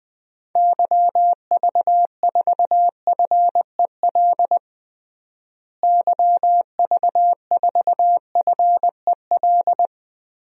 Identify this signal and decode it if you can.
Morse code, heard as a keyed tone, about 20 wpm, 700 Hz. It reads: YV4FEL YV4FEL
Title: DXCC-100 Call Signs - 6 Characters - Repeated 3x - 20wpm